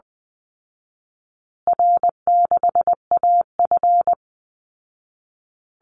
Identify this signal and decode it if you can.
Morse code, heard as a keyed tone, about 20 wpm, 700 Hz. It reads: R6AF
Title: Call Signs Repeated 2x - 20wpm